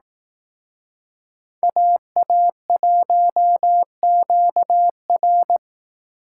Morse code, heard as a keyed tone, about 18 wpm, 700 Hz. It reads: AA1QR
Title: Speed Racing - USA 2x2 Call Signs 15wpm